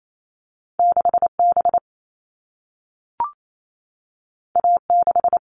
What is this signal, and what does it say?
Morse code, heard as a keyed tone, about 28 wpm, 700 Hz. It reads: 6B A6
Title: All Cut Numbers - 2-Digit 28wpm